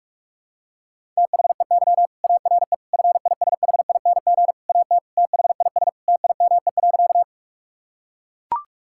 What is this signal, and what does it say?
Morse code, heard as a keyed tone, about 41 wpm, 700 Hz. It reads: THEY ARE FISHING AT THIS TIME.
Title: Sentences from Top 1000 Words - Review Part 5 - 45wpm